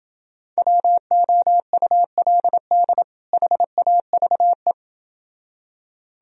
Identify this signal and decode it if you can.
Morse code, heard as a keyed tone, about 29 wpm, 700 Hz. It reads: WOULD HAVE